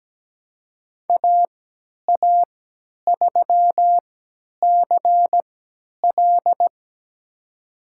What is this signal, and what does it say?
Morse code, heard as a keyed tone, about 17 wpm, spaced as for 10 wpm, 700 Hz. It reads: AA3CL
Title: USA Call Signs - 3x Character Spacing - 17wpm